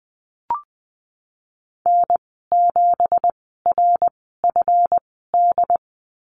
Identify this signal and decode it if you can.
Morse code, heard as a keyed tone, about 20 wpm, spaced as for 15 wpm, 700 Hz. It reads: N7RFD
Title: 179 Lesson 29g - Slash - Call Signs 20wpm